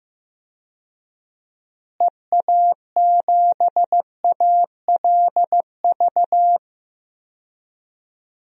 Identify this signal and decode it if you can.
Morse code, heard as a keyed tone, about 15 wpm, 700 Hz. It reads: EA7ALV
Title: DXCC-100 Call Signs - 6 Characters - Repeated 3x - 15wpm